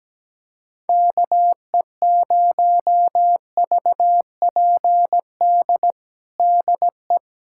Morse code, heard as a keyed tone, about 17 wpm, 700 Hz. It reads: KE0VPD DE
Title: USA Callsign DE Callsign - 6 Characters - 17wpm